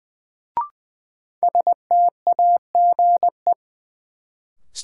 Morse code, heard as a keyed tone, about 20 wpm, 700 Hz. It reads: STAGE
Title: Sentences from Top 1200 Words - New Word Drill Alt - 20wpm